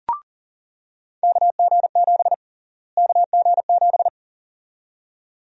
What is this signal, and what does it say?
Morse code, heard as a keyed tone, about 40 wpm, 700 Hz. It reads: KG7 KG7
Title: USA Call Sign Prefixes - Repeated 2x - 40wpm